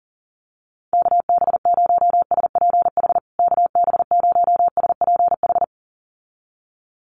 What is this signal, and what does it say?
Morse code, heard as a keyed tone, about 40 wpm, 700 Hz. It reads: KB0SPH KB0SPH
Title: USA Call Signs Repeated 2x 40wpm